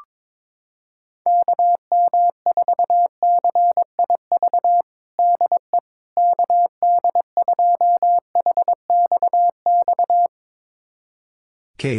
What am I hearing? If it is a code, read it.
Morse code, heard as a keyed tone, about 22 wpm, 700 Hz. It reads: KM4CIV DE KD2HXX
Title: USA Callsign DE Callsign 22wpm